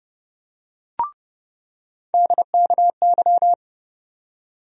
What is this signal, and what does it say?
Morse code, heard as a keyed tone, about 30 wpm, 700 Hz. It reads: DKY
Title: USA Call Sign Postfixes - 3 Letters 30wpm